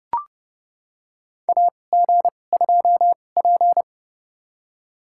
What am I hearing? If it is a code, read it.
Morse code, heard as a keyed tone, about 31 wpm, spaced as for 22 wpm, 700 Hz. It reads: AG2P
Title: USA Call Signs - 2x Character Spacing - 30wpm